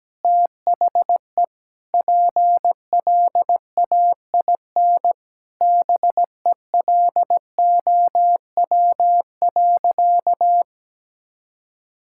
Morse code, heard as a keyed tone, about 17 wpm, 700 Hz. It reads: THE PLAIN BELOW.
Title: Sentences from Top 1000 Words - Review Part 5 - 17wpm